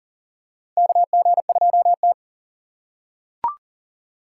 Morse code, heard as a keyed tone, about 40 wpm, 700 Hz. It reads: KG2T
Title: USA Call Signs - 6x Character Spacing - 40wpm